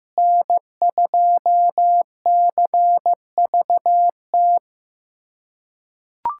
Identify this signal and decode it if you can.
Morse code, heard as a keyed tone, about 15 wpm, 700 Hz. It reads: N2CVT
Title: USA Callsign DE Callsign - 5 Characters - 15wpm